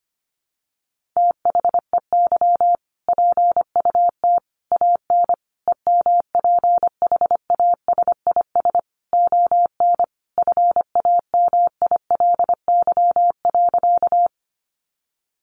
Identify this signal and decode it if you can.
Morse code, heard as a keyed tone, about 25 wpm, 700 Hz. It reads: THEY PUT AN EMPHASIS ON FAMILY.
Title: Sentences from Top 1400 Words 25wpm